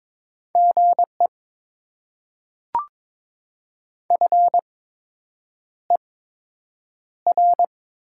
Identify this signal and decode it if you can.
Morse code, heard as a keyed tone, about 22 wpm, spaced as for 8 wpm, 700 Hz. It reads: GE FER
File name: Top 20 QSO Elements - 8x Character Spacing - 22wpm